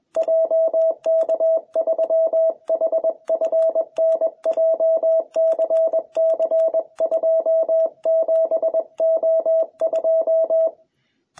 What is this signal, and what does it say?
Morse code, heard as a keyed tone, about 21 wpm, 660 Hz. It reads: JK3HFN JCC27O2